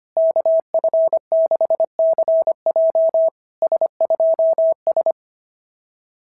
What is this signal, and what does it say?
Morse code, heard as a keyed tone, about 24 wpm, 650 Hz. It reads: KF6CJ S2S
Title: SOTA QSO Practice 25wpm